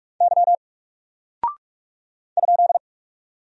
Morse code, heard as a keyed tone, about 44 wpm, 700 Hz. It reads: Y ?
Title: Single Character (Letter:Number:Punctuation:Prosign) Alt 45wpm